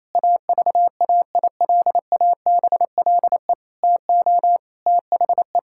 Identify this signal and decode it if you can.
Morse code, heard as a keyed tone, about 28 wpm, 700 Hz. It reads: AVAILABLE TO THE
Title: Sentences from Top 1000 Words - Review Part 6 - 28wpm